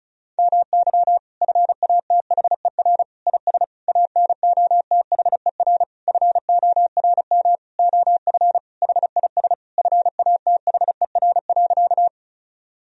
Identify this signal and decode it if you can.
Morse code, heard as a keyed tone, about 36 wpm, 700 Hz. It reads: MY FATHER IS ANOTHER FORM OF HIS FATHER.
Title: Sentences from Top 1000 Words - Review Part 1 - 35wpm